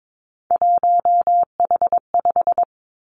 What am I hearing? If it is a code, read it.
Morse code, heard as a keyed tone, about 22 wpm, 700 Hz. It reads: 1H5
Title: Three Challenging Letter-Number Combinations 22wpm